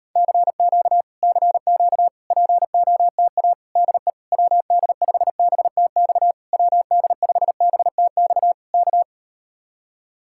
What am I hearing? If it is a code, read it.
Morse code, heard as a keyed tone, about 38 wpm, 700 Hz. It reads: CQ CQ POTA DE WD5BTX WD5BTX K